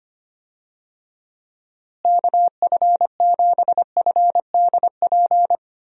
Morse code, heard as a keyed tone, about 25 wpm, 700 Hz. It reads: KF7FDP
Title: USA Call Signs - 6 Characters - Repeated 3x - 25wpm